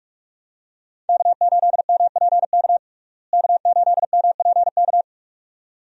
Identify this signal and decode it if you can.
Morse code, heard as a keyed tone, about 44 wpm, 700 Hz. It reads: K8MPK K8MPK
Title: USA Call Signs - 5 Characters - Repeated 2x - 45wpm